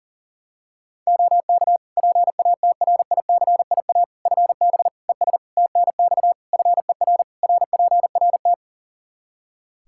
Morse code, heard as a keyed tone, about 40 wpm, 700 Hz. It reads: OK PATRICIA FB ES TNX FER RPRT
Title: Practice QSO - Exchange 2 - Info - 40wpm